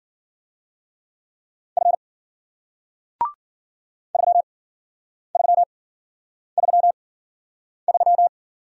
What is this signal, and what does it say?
Morse code, heard as a keyed tone, about 53 wpm, spaced as for 9 wpm, 700 Hz. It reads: F 3333